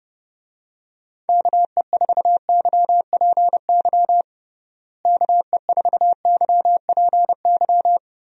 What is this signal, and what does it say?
Morse code, heard as a keyed tone, about 30 wpm, 700 Hz. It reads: KE4YPY KE4YPY
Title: USA Call Signs - 6 Characters - Repeated 3x - 30wpm